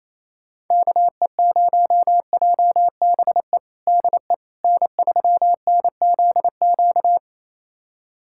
Morse code, heard as a keyed tone, about 28 wpm, 700 Hz. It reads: KE0JBE DE N3NZQ
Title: USA Callsign DE Callsign 28wpm